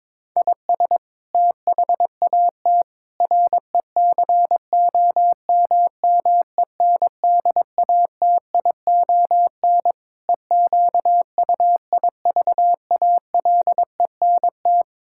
Morse code, heard as a keyed tone, about 22 wpm, 700 Hz. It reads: IS THAT RECOMMENDATION EQUIVALENT
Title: Sentences from Top 1600 Words 22wpm